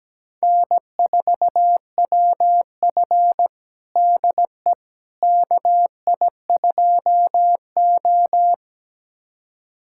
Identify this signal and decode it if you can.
Morse code, heard as a keyed tone, about 17 wpm, 700 Hz. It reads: N4WF DE KI2O
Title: USA Callsign DE Callsign - 4 Characters - 17wpm